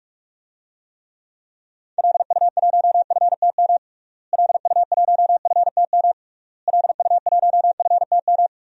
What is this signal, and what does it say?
Morse code, heard as a keyed tone, about 41 wpm, 700 Hz. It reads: LU1FTM LU1FTM LU1FTM
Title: DXCC-100 Call Signs - 6 Characters - Repeated 3x - 45wpm